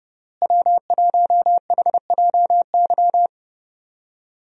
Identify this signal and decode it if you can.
Morse code, heard as a keyed tone, about 30 wpm, 700 Hz. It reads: W1HJY
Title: USA Call Signs - 5 Characters - Repeated 2x - 30wpm